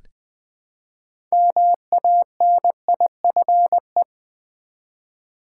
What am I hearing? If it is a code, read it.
Morse code, heard as a keyed tone, about 20 wpm, 700 Hz. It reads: MANIFE
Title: Build A Word - 3 Letters At A Time - 8-Letter Words - 20wpm